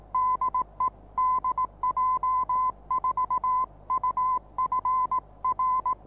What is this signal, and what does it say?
Morse code, heard as a keyed tone, about 18 wpm, 1020 Hz. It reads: DE DJ4UF R
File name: repos / Morsekurs / DJ4UF Morsen26